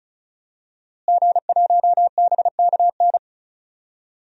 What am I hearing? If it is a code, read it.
Morse code, heard as a keyed tone, about 34 wpm, 700 Hz. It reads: G1BKN
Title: DXCC-100 Call Signs - 5 Characters - Repeated 3x - 35wpm